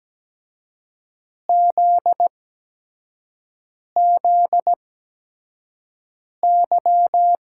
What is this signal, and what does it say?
Morse code, heard as a keyed tone, about 17 wpm, spaced as for 5 wpm, 700 Hz. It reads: ZZY
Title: USA Call Signs - 6 Characters - 8x Character Spacing - 17wpm